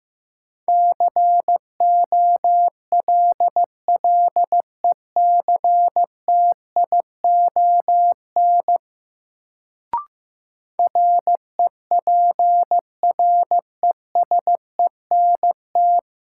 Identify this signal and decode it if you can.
Morse code, heard as a keyed tone, about 16 wpm, 700 Hz. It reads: COLLECTION REPRESENT
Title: Sentences from Top 1200 Words - New Word Drill - 15wpm